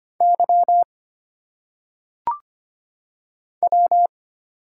Morse code, heard as a keyed tone, about 25 wpm, 700 Hz. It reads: Y W
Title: Single Letters - Phonetic Pronunciation 25wpm